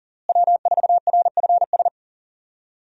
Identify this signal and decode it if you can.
Morse code, heard as a keyed tone, about 37 wpm, 700 Hz. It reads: W4RFS
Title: USA Call Signs - 5 Characters - Repeated 2x - 40wpm